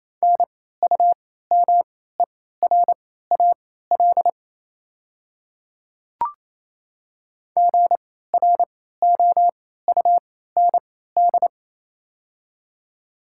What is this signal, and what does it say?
Morse code, heard as a keyed tone, about 28 wpm, spaced as for 16 wpm, 700 Hz. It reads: NUMERAL GROUND
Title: 3x Character Spacing - Top 500 Words - 28wpm